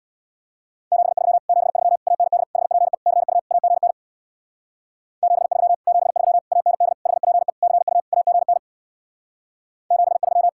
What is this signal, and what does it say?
Morse code, heard as a keyed tone, about 61 wpm, 700 Hz. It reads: 73 73 AND HPE CU AGN 73 73 AND HPE CU AGN 73
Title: Speed-Racing Common QSO Phrases - 50wpm